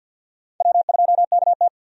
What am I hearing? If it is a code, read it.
Morse code, heard as a keyed tone, about 49 wpm, 700 Hz. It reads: W2KT
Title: USA Call Signs - 4 Characters - 50wpm